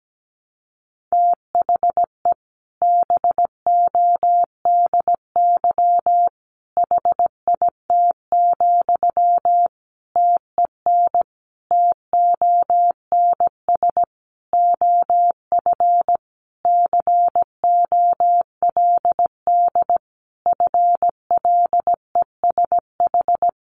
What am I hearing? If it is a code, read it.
Morse code, heard as a keyed tone, about 17 wpm, 700 Hz. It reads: THE BODY HIT, TEN TONS OF COLD FLESH
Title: A Sound Of Thunder 17wpm